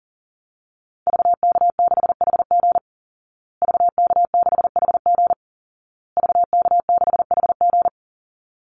Morse code, heard as a keyed tone, about 40 wpm, 700 Hz. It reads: VK6HG VK6HG VK6HG
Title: Call Signs Repeated 2x - 40wpm